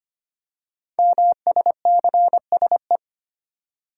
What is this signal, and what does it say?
Morse code, heard as a keyed tone, about 25 wpm, 700 Hz. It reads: MSCSE
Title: Code Groups - 5 Letter:Numbers - 25wpm